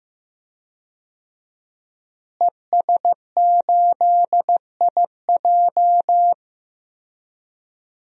Morse code, heard as a keyed tone, about 16 wpm, 700 Hz. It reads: ES8IJ
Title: DXCC-100 Call Signs - 5 Characters - Repeated 2x - 15wpm